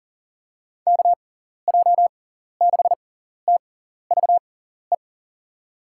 Morse code, heard as a keyed tone, about 40 wpm, spaced as for 14 wpm, 700 Hz. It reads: KJ6TVE
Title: USA Call Signs - 6 Characters - 6x Character Spacing - 40wpm